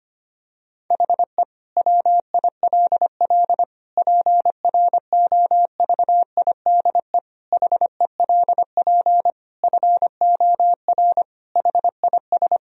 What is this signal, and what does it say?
Morse code, heard as a keyed tone, about 25 wpm, 700 Hz. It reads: HE WILL PROVIDE HELP FOR HIS
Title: Sentences from Top 1000 Words - Review Part 3 - 25wpm